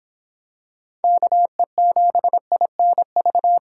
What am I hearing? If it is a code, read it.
Morse code, heard as a keyed tone, about 25 wpm, 700 Hz. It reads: KE7INV